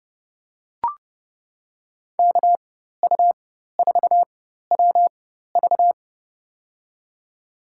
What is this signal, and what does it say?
Morse code, heard as a keyed tone, about 32 wpm, spaced as for 14 wpm, 700 Hz. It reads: KU4WV
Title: USA Call Signs - 5 Characters - 4x Character Spacing - 30wpm